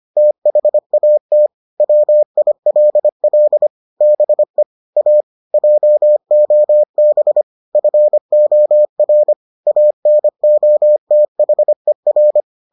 Morse code, heard as a keyed tone, about 25 wpm, 600 Hz. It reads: THAT WILL BE A JOB FOR ANOTHER